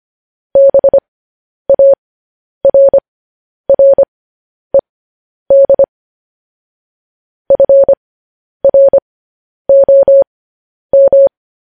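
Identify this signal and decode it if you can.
Morse code, heard as a keyed tone, about 25 wpm, spaced as for 10 wpm, 550 Hz. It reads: BARRED FROM
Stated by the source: Morse code news headlines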